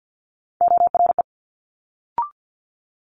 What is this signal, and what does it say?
Morse code, heard as a keyed tone, about 50 wpm, 700 Hz. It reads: ORE
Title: Multi-letter Phonemes Quick Fire 50wpm